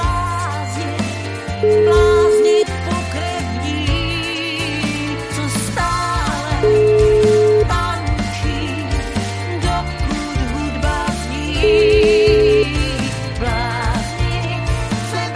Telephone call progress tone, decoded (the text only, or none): ringback tone